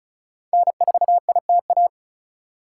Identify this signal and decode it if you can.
Morse code, heard as a keyed tone, about 36 wpm, 700 Hz. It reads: N4ITA
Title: USA Call Signs - 5 Characters - 4x Character Spacing - 35wpm